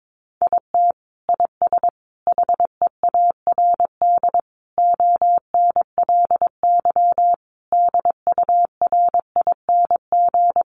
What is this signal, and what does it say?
Morse code, heard as a keyed tone, about 22 wpm, 700 Hz. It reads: IT IS HEARD ONLY DURING
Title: Sentences from Top 1200 Words 22wpm